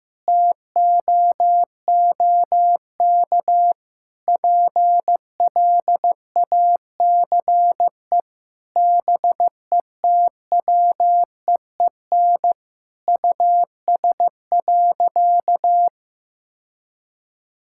Morse code, heard as a keyed tone, about 16 wpm, 700 Hz. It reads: TOOK PLACE BETWEEN US.
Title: Sentences from Top 1000 Words - Review Part 6 - 15wpm